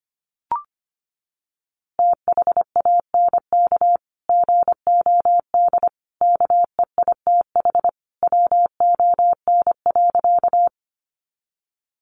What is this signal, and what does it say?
Morse code, heard as a keyed tone, about 25 wpm, 700 Hz. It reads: THANK GOD KEITH WON.
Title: A Sound Of Thunder 25wpm